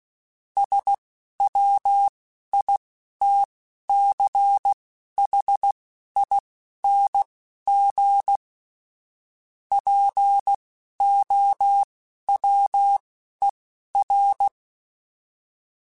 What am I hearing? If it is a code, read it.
Morse code, heard as a keyed tone, about 16 wpm, spaced as for 11 wpm, 800 Hz. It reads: SWITCHING POWER
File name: Technical Subjects / Technical Subjects-mp3s / Sw Pwr Sup @10